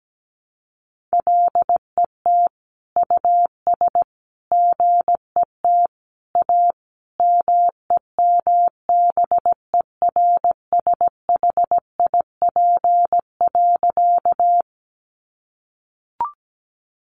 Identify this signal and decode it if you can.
Morse code, heard as a keyed tone, about 17 wpm, 700 Hz. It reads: LET US GET A MEMBERSHIP.
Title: Sentences from Top 1400 Words 17wpm